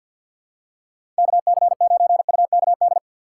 Morse code, heard as a keyed tone, about 54 wpm, 700 Hz. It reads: KC9UKD
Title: USA Call Signs - 6 Characters - 3x Character Spacing - 50wpm